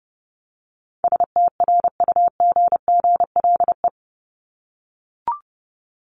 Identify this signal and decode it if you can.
Morse code, heard as a keyed tone, about 30 wpm, 700 Hz. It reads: STRUGGLE
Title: Top 1751-2000 Words 30wpm